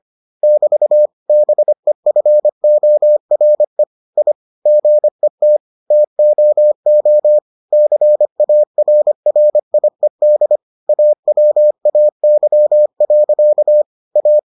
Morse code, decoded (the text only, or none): <BT> BEFORE I GET TOO CARRIED AWAY. A